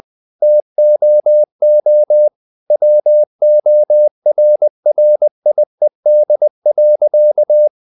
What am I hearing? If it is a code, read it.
Morse code, heard as a keyed tone, about 20 wpm, 600 Hz. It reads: TOO WORRIED.
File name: the old man and the sea part 2 20wpm